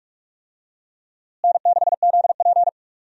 Morse code, heard as a keyed tone, about 46 wpm, 700 Hz. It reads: N6ZP
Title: USA Call Signs - 4 Characters - Repeated 3x - 45wpm